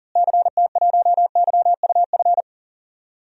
Morse code, heard as a keyed tone, about 40 wpm, 700 Hz. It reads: CT1YUF
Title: DXCC-100 Call Signs - 6 Characters - Repeated 3x - 40wpm